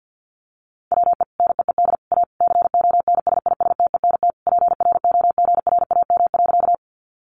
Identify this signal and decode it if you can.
Morse code, heard as a keyed tone, about 50 wpm, 700 Hz. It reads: WE NEED A CONSISTENT PROGRAM.
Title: Sentences from Top 1500 Words 50wpm